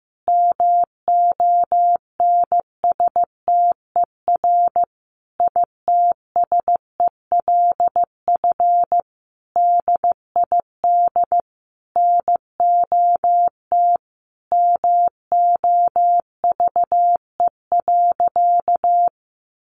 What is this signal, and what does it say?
Morse code, heard as a keyed tone, about 14 wpm, 700 Hz. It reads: MONSTER ITSELF DID NOT MOVE.
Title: A Sound Of Thunder 15wpm